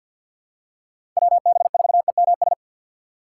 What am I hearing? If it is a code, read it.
Morse code, heard as a keyed tone, about 52 wpm, 700 Hz. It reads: WB4EMS
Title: USA Call Signs - 6 Characters - 4x Character Spacing - 50wpm